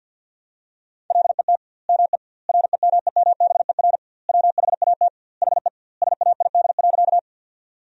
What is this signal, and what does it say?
Morse code, heard as a keyed tone, about 50 wpm, 700 Hz. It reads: LET ME REMEMBER WHAT HE SAID.
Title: 2x Word-Spacing - Sentences from Top 400 Words - 50wpm